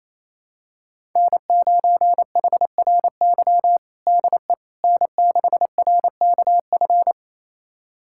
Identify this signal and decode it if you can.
Morse code, heard as a keyed tone, about 28 wpm, 700 Hz. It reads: N9HRY DE N6RKF
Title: USA Callsign DE Callsign - 5 Characters - 28wpm